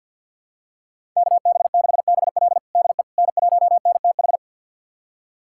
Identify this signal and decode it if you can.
Morse code, heard as a keyed tone, about 52 wpm, 700 Hz. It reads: KB6BL DE N1NTH
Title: USA Callsign DE Callsign - 5 Characters - 50wpm